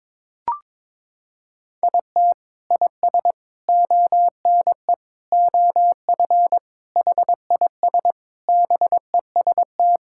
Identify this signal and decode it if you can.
Morse code, heard as a keyed tone, about 22 wpm, 700 Hz. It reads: IT IS ONE OF HIS BEST
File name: practice / 22wpm / Sentences with 3-Word Phrases - 22wpm